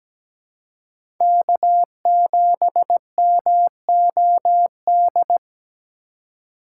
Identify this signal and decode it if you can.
Morse code, heard as a keyed tone, about 17 wpm, 700 Hz. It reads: K7MOD
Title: USA Call Signs - 3x Character Spacing - 17wpm